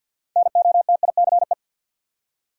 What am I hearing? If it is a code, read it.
Morse code, heard as a keyed tone, about 53 wpm, 700 Hz. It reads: NOTICE
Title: Six Letter Words - 50wpm